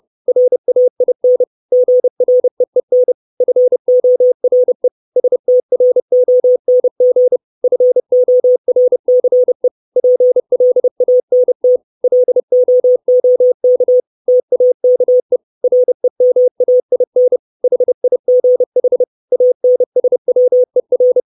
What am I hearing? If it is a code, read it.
Morse code, heard as a keyed tone, about 30 wpm, 500 Hz. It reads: RAIN GREEN FORE STRONG FORCE PLANT LOOK TAKE REMAIN HIGH ANSWER
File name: 632 Words 30WPM